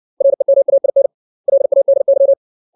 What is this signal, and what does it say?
Morse code, heard as a keyed tone, about 60 wpm, 550 Hz. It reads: REMAIN LADY
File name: 632 words / 632 Words 60WPM